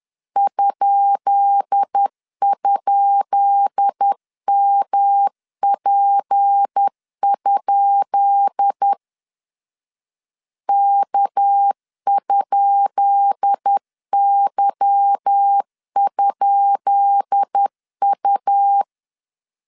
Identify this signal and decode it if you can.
Morse code, heard as a keyed tone, about 10 wpm, 800 Hz. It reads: ??MP? K?Y?U
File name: LEZIONI / 10 SIGNS / 10B INT 10wpm 2844